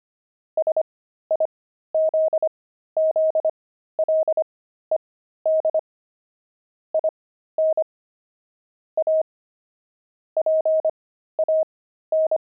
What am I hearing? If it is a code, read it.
Morse code, heard as a keyed tone, about 25 wpm, spaced as for 13 wpm, 630 Hz. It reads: SIZZLED IN A PAN
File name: pangram-fs-13-cs-25-steaks